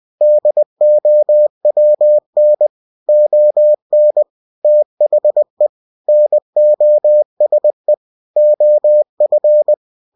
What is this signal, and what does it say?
Morse code, heard as a keyed tone, about 20 wpm, 600 Hz. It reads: DOWN ON THE NOSE OF
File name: cw-books / winnie-the-pooh / winnie-20wpm-page-0009